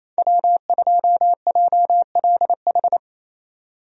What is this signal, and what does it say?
Morse code, heard as a keyed tone, about 28 wpm, 700 Hz. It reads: W2JLH
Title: USA Call Signs - 5 Characters - 28wpm